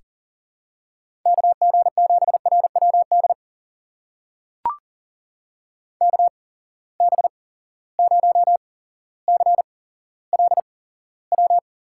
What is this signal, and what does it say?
Morse code, heard as a keyed tone, about 41 wpm, spaced as for 17 wpm, 700 Hz. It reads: KG7RWD KB0CLW
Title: USA Call Signs - 6 Characters - 8x Character Spacing - 40wpm